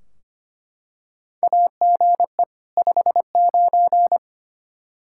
Morse code, heard as a keyed tone, about 25 wpm, 700 Hz. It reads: AGE 59
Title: Practice QSO - Exchange 3 - More Info - 25wpm